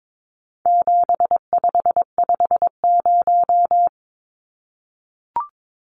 Morse code, heard as a keyed tone, about 22 wpm, 700 Hz. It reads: 7550
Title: Numbers - 4-Digit 22wpm